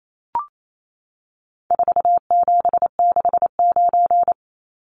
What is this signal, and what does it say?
Morse code, heard as a keyed tone, about 28 wpm, 700 Hz. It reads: 4769
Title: Numbers - 4-Digit 28wpm